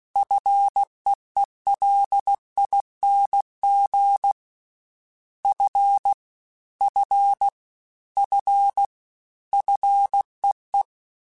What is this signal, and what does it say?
Morse code, heard as a keyed tone, about 16 wpm, 800 Hz. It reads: FEELING F F F FEE
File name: Single F Prefixes 3x2 Twice @15